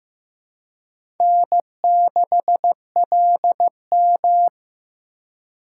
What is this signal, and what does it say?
Morse code, heard as a keyed tone, about 18 wpm, 700 Hz. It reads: N6LM
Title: Speed Racing - USA 1x2 Call Signs 15wpm